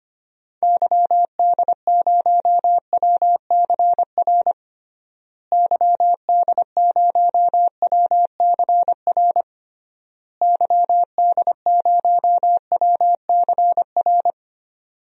Morse code, heard as a keyed tone, about 25 wpm, 700 Hz. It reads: YD0WCR YD0WCR YD0WCR
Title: DXCC-100 Call Signs - 6 Characters - Repeated 3x - 25wpm